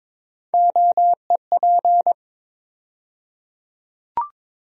Morse code, heard as a keyed tone, about 22 wpm, 700 Hz. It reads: OEP
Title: USA Call Sign Postfixes - 3 Letters 22wpm